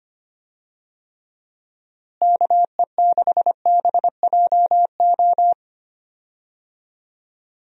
Morse code, heard as a keyed tone, about 25 wpm, 700 Hz. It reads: KE6BJO
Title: USA Call Signs - 6 Characters - Repeated 3x - 25wpm